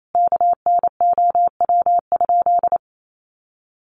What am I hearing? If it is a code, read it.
Morse code, heard as a keyed tone, about 28 wpm, 700 Hz. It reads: KNOW?
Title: Sentences from Top 100 Words 28wpm